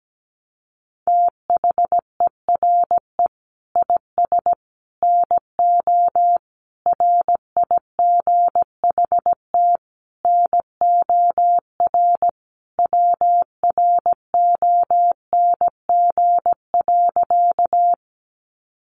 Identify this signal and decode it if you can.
Morse code, heard as a keyed tone, about 17 wpm, 700 Hz. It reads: THERE IS NO RIGHT NOR WRONG.
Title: Sentences from Top 1500 Words 17wpm